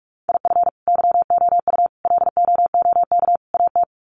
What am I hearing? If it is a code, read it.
Morse code, heard as a keyed tone, about 44 wpm, 700 Hz. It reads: IF YOU LOOK AT
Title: Sets of 4 Words 45wpm